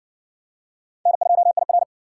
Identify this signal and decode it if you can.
Morse code, heard as a keyed tone, about 56 wpm, 700 Hz. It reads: N2IR